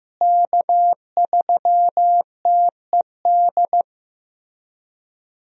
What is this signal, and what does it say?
Morse code, heard as a keyed tone, about 16 wpm, 700 Hz. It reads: K3TED
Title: USA Call Signs - 5 Characters - Repeated 3x - 15wpm